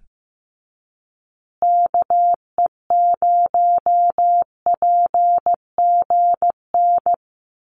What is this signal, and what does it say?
Morse code, heard as a keyed tone, about 15 wpm, 700 Hz. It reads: KE0PGN
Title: USA Call Signs Repeated 3x 15wpm